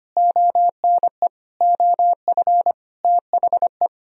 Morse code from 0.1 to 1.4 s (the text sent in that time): ONE